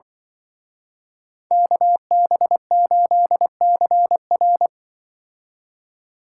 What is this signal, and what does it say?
Morse code, heard as a keyed tone, about 23 wpm, 700 Hz. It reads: KB8CR
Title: Speed Racing - USA 2x2 Call Signs 20wpm